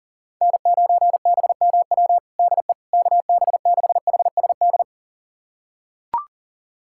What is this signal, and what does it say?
Morse code, heard as a keyed tone, about 40 wpm, 700 Hz. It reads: N9BMW DE KB6HSD
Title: USA Callsign DE Callsign 40wpm